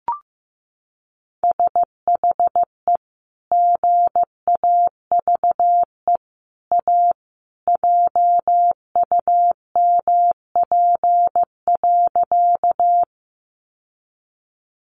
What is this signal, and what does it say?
Morse code, heard as a keyed tone, about 15 wpm, 700 Hz. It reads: SHE GAVE A JUMP.
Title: Sentences from Top 1300 Words 15wpm